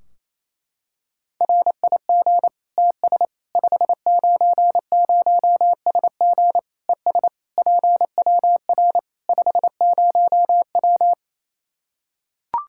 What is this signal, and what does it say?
Morse code, heard as a keyed tone, about 27 wpm, 700 Hz. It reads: RIG TS 590SG ES PWR 50W
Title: Practice QSO - Exchange 2 - Info - 28wpm